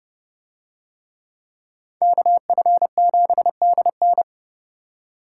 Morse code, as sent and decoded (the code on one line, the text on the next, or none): -.- ..-. --... -.. -.
KF7DN